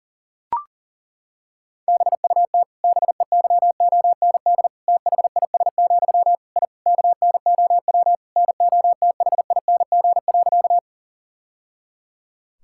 Morse code, heard as a keyed tone, about 40 wpm, 700 Hz. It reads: BUT BEYOND THIS, I KNOW NOTHING.
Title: Sentences from Top 1000 Words - Review Part 6 - 40wpm